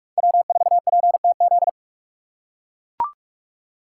Morse code, heard as a keyed tone, about 44 wpm, 700 Hz. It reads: W4PTZ